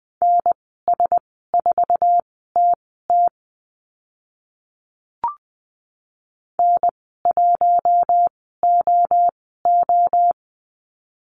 Morse code, heard as a keyed tone, about 20 wpm, spaced as for 14 wpm, 700 Hz. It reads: NS4TT N1OO
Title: 048 Lesson 9g - Number 4 - Call Signs 20wpm